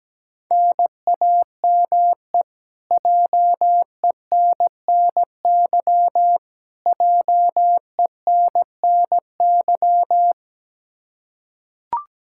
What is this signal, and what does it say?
Morse code, heard as a keyed tone, about 17 wpm, 700 Hz. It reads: NAME JENNY JENNY
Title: Practice QSO - Exchange 1 - Beginning - 17wpm